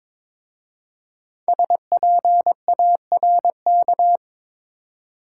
Morse code, heard as a keyed tone, about 22 wpm, 700 Hz. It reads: SPARK
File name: 5 Letter Words - First 3 Letters Given - 22wpm